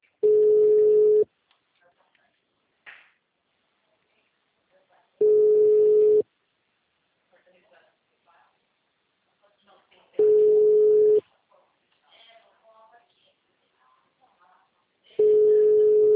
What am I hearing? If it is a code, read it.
Ringback tone (Europe, 425 Hz), with speech.
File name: out- -2006- - - .7